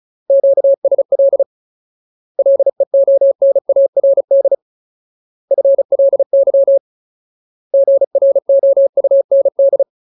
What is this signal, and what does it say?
Morse code, heard as a keyed tone, about 34 wpm, 550 Hz. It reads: QSL LEONARD FLY GROUND
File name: Ultimate Word List / 35WPM Ultimate